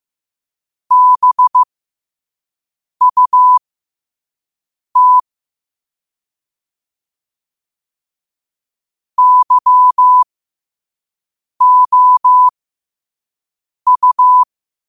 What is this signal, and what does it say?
Morse code, heard as a keyed tone, about 15 wpm, spaced as for 5 wpm, 1000 Hz. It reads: BUT YOU
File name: qotdfiles / 5wpm